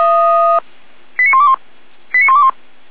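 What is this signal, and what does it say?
Five-tone selective call (ZVEI), heard as quick runs of five tones, tones of about 70 ms. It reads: 89121 89121
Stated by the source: Five-tone ZVEI selective call received by SDR.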